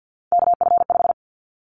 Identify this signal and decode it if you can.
Morse code, heard as a keyed tone, about 50 wpm, 700 Hz. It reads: KF5
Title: Callsign Prefixes for Top 25 DXCC - 50wpm